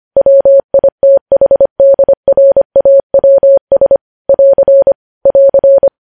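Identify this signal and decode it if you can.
Morse code, heard as a keyed tone, about 25 wpm, 550 Hz. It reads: WITHDRAWS <AR> <AR>
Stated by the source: Morse code news headlines sent at 25 wpm.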